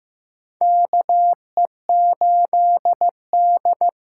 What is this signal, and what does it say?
Morse code, heard as a keyed tone, about 15 wpm, 700 Hz. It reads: KE8D
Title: USA Call Signs - 6 Characters - 8x Character Spacing - 15wpm